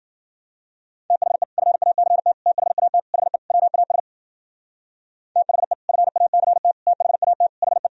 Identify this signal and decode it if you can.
Morse code, heard as a keyed tone, about 50 wpm, 700 Hz. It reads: THE FACT THAT HE WAS THE FACT THAT HE